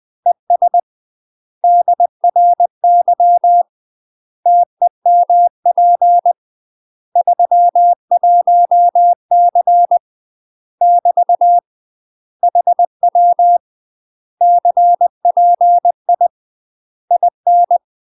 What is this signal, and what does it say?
Morse code, heard as a keyed tone, about 20 wpm, 700 Hz. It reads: ES DRY TEMP 31C <BT> HW CPI IN
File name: UK MASTER QSOs TRAINING / 20WPM / UK MASTER QSOs 0050